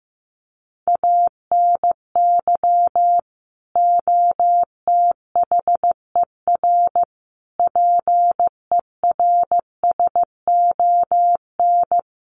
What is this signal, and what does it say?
Morse code, heard as a keyed tone, about 15 wpm, 700 Hz. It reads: ANY OTHER PERSON
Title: Sentences from Top 1100 Words 15wpm